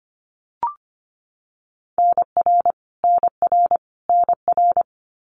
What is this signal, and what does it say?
Morse code, heard as a keyed tone, about 25 wpm, 700 Hz. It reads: NR NR NR
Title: Abbreviations Repeated 3x - 25wpm